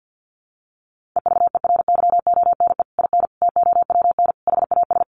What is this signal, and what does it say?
Morse code, heard as a keyed tone, about 50 wpm, 700 Hz. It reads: EVERYONE IN TOWN HAS
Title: Sentences from Top 1500 Words 50wpm